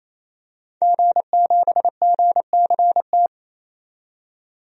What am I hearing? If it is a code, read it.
Morse code, heard as a keyed tone, about 28 wpm, 700 Hz. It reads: G7GCT
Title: DXCC-100 Call Signs - 5 Characters - 28wpm